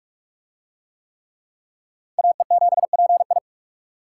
Morse code, heard as a keyed tone, about 44 wpm, 700 Hz. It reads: AE7PI